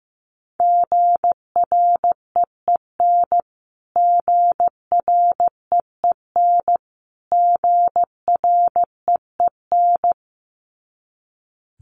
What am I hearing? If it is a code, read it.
Morse code, heard as a keyed tone, about 15 wpm, 700 Hz. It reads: GREEN GREEN GREEN
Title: Top 500 Words Repeated 15wpm